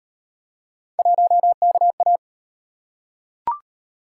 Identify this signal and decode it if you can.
Morse code, heard as a keyed tone, about 35 wpm, 700 Hz. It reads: 1KA